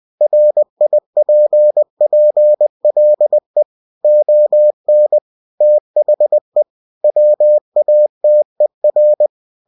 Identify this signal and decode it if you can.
Morse code, heard as a keyed tone, about 20 wpm, 600 Hz. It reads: RIPPLE ON THE WATER
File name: cw-books / winnie-the-pooh / winnie-20wpm-page-0092